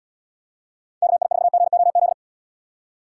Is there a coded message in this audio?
Yes: Morse code from 1.0 to 2.1 s, heard as a keyed tone, about 75 wpm, 700 Hz. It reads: KI4GOZ